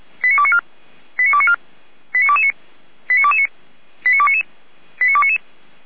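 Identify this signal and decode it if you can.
Five-tone selective call (ZVEI), heard as quick runs of five tones, tones of about 70 ms. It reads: 89294 89294 89229 89229 89299 89299
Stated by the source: SDR-received ZVEI five-tone call, code 89276, archived by feuerwehr-orlamuende (fire brigade Orlamünde) server f85360.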